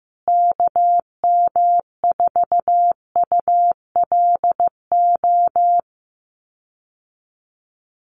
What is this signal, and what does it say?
Morse code, heard as a keyed tone, about 15 wpm, 700 Hz. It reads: KM4ULO
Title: USA Call Signs Repeated 3x 15wpm